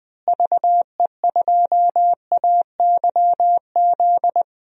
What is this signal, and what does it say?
Morse code, heard as a keyed tone, about 20 wpm, 700 Hz. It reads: VE2AYZ
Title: DXCC-100 Call Signs - 6 Characters - Repeated 3x - 20wpm